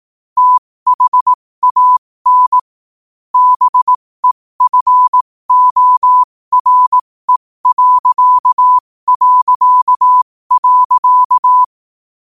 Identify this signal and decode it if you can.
Morse code, heard as a keyed tone, about 18 wpm, spaced as for 15 wpm, 1000 Hz. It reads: THAN BEFORE...
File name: qotdfiles / 15wpm